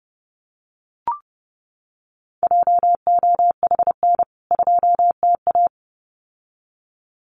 Morse code, heard as a keyed tone, about 30 wpm, 700 Hz. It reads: JOHN 2TA
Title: CWT Exchange 30wpm